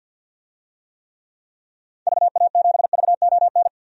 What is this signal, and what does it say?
Morse code, heard as a keyed tone, about 46 wpm, 700 Hz. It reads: VA7VON
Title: DXCC-100 Call Signs - 6 Characters - 50wpm